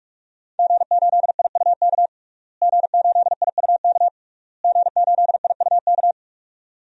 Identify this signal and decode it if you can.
Morse code, heard as a keyed tone, about 45 wpm, 700 Hz. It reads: G8IUK G8IUK G8IUK
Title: DXCC-100 Call Signs - 5 Characters - Repeated 3x - 45wpm